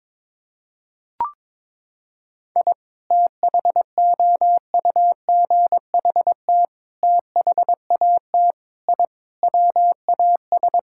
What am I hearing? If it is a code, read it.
Morse code, heard as a keyed tone, about 22 wpm, 700 Hz. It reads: I THOUGHT THAT I WAS
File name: practice / 22wpm / Sentences with 3-Word Phrases - 22wpm